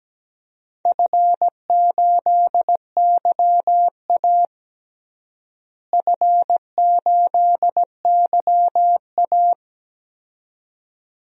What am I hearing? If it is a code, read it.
Morse code, heard as a keyed tone, about 17 wpm, 700 Hz. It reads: F8YA F8YA
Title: DXCC-100 Call Signs - 4 Characters - Repeated 2x - 17wpm